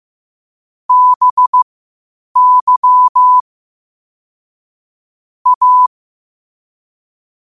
Morse code, heard as a keyed tone, about 15 wpm, spaced as for 8 wpm, 1000 Hz. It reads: BY A
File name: qotdfiles / 8wpm